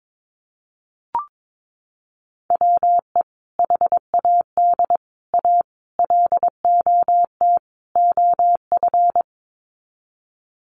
Morse code, heard as a keyed tone, about 22 wpm, 700 Hz. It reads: WE HAD A LOT OF
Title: Sets of 5 Words Encore 22wpm